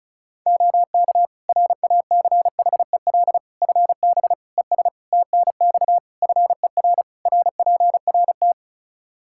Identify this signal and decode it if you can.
Morse code, heard as a keyed tone, about 36 wpm, 700 Hz. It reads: OK RACHEL FB ES TNX FER RPRT
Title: K1USN SST QSO Practice 35wpm